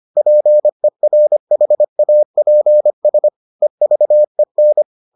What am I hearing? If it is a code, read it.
Morse code, heard as a keyed tone, about 25 wpm, 600 Hz. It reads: PERHAPS EVEN